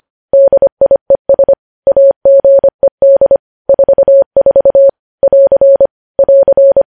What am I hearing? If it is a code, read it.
Morse code, heard as a keyed tone, about 25 wpm, 550 Hz. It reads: DIES AGED 44 <AR> <AR>
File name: podcast / News headlines-1x-25x25